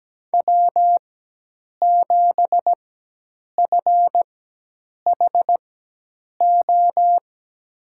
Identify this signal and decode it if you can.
Morse code, heard as a keyed tone, about 17 wpm, spaced as for 8 wpm, 700 Hz. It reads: W7FHO
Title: USA Call Signs - 4x Character Spacing - 17wpm